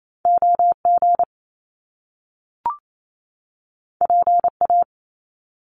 Morse code, heard as a keyed tone, about 28 wpm, 700 Hz. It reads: OG PA
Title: Top Two Letter Combinations in Words 28wpm